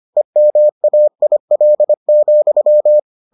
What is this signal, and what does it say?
Morse code, heard as a keyed tone, about 25 wpm, 600 Hz. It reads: EMAIL,